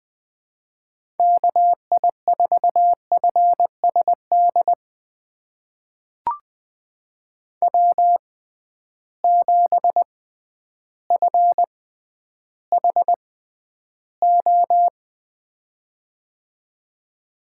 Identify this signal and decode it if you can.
Morse code, heard as a keyed tone, about 20 wpm, spaced as for 11 wpm, 700 Hz. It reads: KI4FSD W7FHO
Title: USA Call Signs - 6x Character Spacing - 20wpm